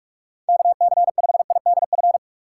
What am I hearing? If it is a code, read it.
Morse code, heard as a keyed tone, about 46 wpm, 700 Hz. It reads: KC5IDF
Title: USA Call Signs - 6 Characters - 2x Character Spacing - 45wpm